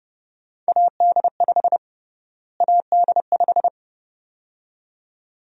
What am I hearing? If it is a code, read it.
Morse code, heard as a keyed tone, about 30 wpm, 700 Hz. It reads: AD5 AD5
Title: USA Call Sign Prefixes - Repeated 2x - 30wpm